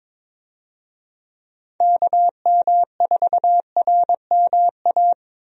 Morse code, heard as a keyed tone, about 22 wpm, 700 Hz. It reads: KM4RMA
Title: USA Call Signs - 6 Characters - Repeated 2x - 22wpm